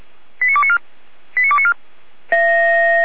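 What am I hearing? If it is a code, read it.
Five-tone selective call (ZVEI), heard as quick runs of five tones, tones of about 70 ms. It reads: 89294 89294
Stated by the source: ZVEI five-tone call received with an SDR for the Feuerwehr Orlamünde fire brigade.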